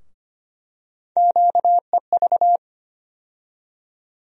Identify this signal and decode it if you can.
Morse code, heard as a keyed tone, about 25 wpm, 700 Hz. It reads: QEV
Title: USA Call Sign Postfixes - 3 Letters 25wpm